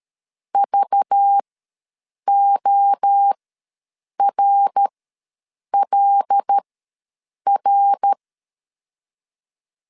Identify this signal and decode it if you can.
Morse code, heard as a keyed tone, about 13 wpm, spaced as for 7 wpm, 800 Hz. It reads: VORLR
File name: LEZIONI / 5 LBVC / 5 LBVC ws 12wpm 3159